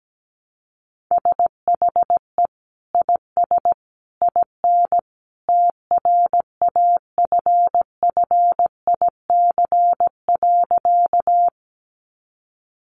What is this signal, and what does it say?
Morse code, heard as a keyed tone, about 17 wpm, 700 Hz. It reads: SHE IS IN TRAFFIC.
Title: Sentences from Top 1300 Words 17wpm